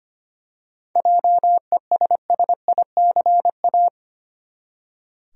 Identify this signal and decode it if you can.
Morse code, heard as a keyed tone, about 25 wpm, 700 Hz. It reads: JESSICA
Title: USA Common Names - 25wpm